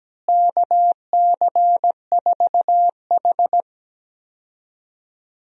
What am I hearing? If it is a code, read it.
Morse code, heard as a keyed tone, about 17 wpm, 700 Hz. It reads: KC4H
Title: USA Call Signs - 4 Characters - Repeated 2x - 17wpm